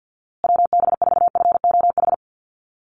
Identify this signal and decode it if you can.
Morse code, heard as a keyed tone, about 49 wpm, 700 Hz. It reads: WB4FOH
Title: USA Call Signs Repeated 3x 50wpm